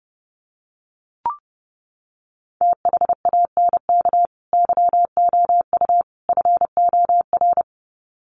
Morse code, heard as a keyed tone, about 30 wpm, 700 Hz. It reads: THANK YOU FOR
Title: Sets of 3 Words 30wpm